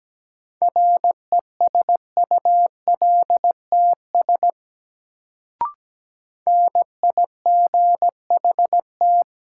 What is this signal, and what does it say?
Morse code, heard as a keyed tone, about 17 wpm, 700 Hz. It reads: RESULTS NIGHT
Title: Sets of 4 Words - New Word Drill - 17wpm